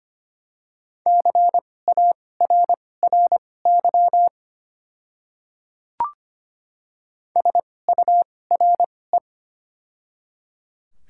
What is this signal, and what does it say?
Morse code, heard as a keyed tone, about 25 wpm, spaced as for 18 wpm, 700 Hz. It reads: CARRY SURE
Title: 2x Character Spacing - Top 300 Words - 25wpm